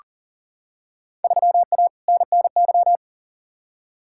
Morse code, heard as a keyed tone, about 40 wpm, 700 Hz. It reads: 3A NNY
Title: Field Day Exchanges - 40wpm